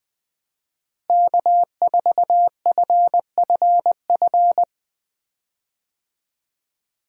Morse code, heard as a keyed tone, about 20 wpm, 700 Hz. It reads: K4FFF
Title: USA Call Signs - 5 Characters - Repeated 3x - 20wpm